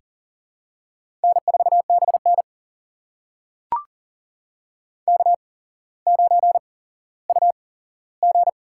Morse code, heard as a keyed tone, about 40 wpm, spaced as for 17 wpm, 700 Hz. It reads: N4BN K9UG
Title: USA Call Signs - 4 Characters - 8x Character Spacing - 40wpm